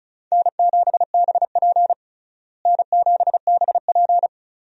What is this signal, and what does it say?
Morse code, heard as a keyed tone, about 34 wpm, 700 Hz. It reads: N7BP N7BP
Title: USA Call Signs - 4 Characters - Repeated 2x - 35wpm